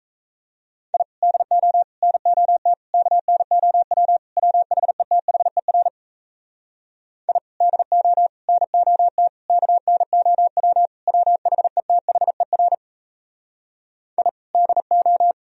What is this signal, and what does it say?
Morse code, heard as a keyed tone, about 38 wpm, 700 Hz. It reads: I DO NOT KNOW WHETHER I DO NOT KNOW WHETHER I DO